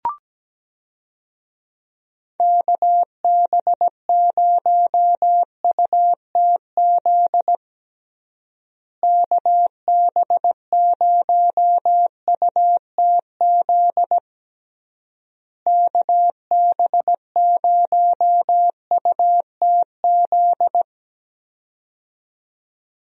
Morse code, heard as a keyed tone, about 17 wpm, 700 Hz. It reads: KB0UTZ KB0UTZ KB0UTZ
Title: USA Call Signs - 6 Characters - Repeated 3x - 17wpm